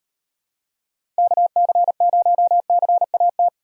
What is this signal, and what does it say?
Morse code, heard as a keyed tone, about 38 wpm, 700 Hz. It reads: KC0CAT